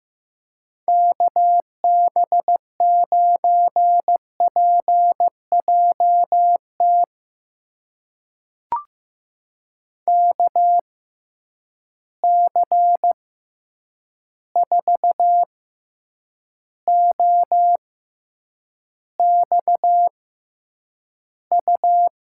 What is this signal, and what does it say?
Morse code, heard as a keyed tone, about 16 wpm, spaced as for 8 wpm, 700 Hz. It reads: KB9PJT KC4OXU
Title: USA Call Signs - 6 Characters - 6x Character Spacing - 15wpm